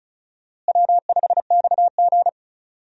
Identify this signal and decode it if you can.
Morse code, heard as a keyed tone, about 34 wpm, 700 Hz. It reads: W5XG
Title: USA Call Signs - 4 Characters - Repeated 2x - 35wpm